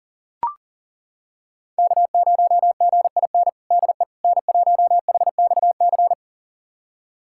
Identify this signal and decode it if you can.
Morse code, heard as a keyed tone, about 40 wpm, 700 Hz. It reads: K0GIN DE N1HXC
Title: USA Callsign DE Callsign - 5 Characters - 40wpm